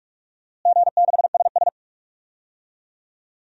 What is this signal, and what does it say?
Morse code, heard as a keyed tone, about 46 wpm, 700 Hz. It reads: G6SS
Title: DXCC-100 Call Signs - 4 Characters - Repeated 3x - 45wpm